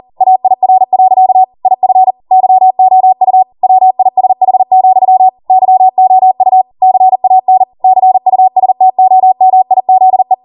Morse code, heard as a keyed tone, about 40 wpm, 750 Hz. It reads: AIR. IF YOU WISH, YOU CAN CUSTOMIZE